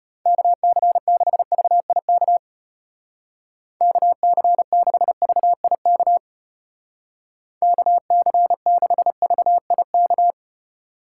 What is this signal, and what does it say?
Morse code, heard as a keyed tone, about 31 wpm, 700 Hz. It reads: KC6VIK KC6VIK KC6VIK